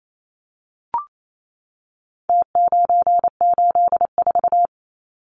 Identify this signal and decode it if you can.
Morse code, heard as a keyed tone, about 28 wpm, 700 Hz. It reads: T984
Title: Numbers - Contest Style - 4-Digit 28wpm